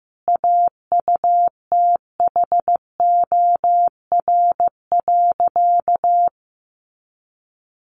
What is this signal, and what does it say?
Morse code, heard as a keyed tone, about 15 wpm, 700 Hz. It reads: AUTHOR.
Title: Sentences from Top 1200 Words 15wpm